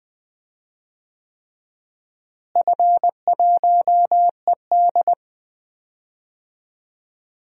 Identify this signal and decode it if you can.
Morse code, heard as a keyed tone, about 21 wpm, 700 Hz. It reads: F1ED
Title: DXCC-100 Call Signs - 4 Characters - Repeated 3x - 20wpm